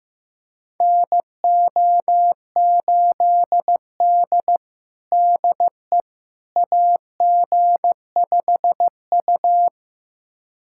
Morse code, heard as a keyed tone, about 15 wpm, 700 Hz. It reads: NO8D DE AG5U
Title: USA Callsign DE Callsign - 4 Characters - 15wpm